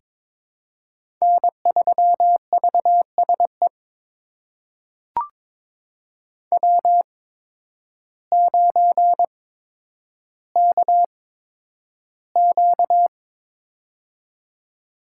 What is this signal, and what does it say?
Morse code, heard as a keyed tone, about 22 wpm, spaced as for 10 wpm, 700 Hz. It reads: N3VSE W9KQ
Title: USA Call Signs - 8x Character Spacing - 22wpm